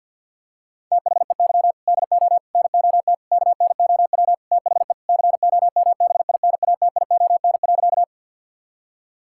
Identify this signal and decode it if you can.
Morse code, heard as a keyed tone, about 50 wpm, 700 Hz. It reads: THEY DO NOT KNOW THE COMBINATION.
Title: Sentences from Top 1000 Words - Review Part 6 - 50wpm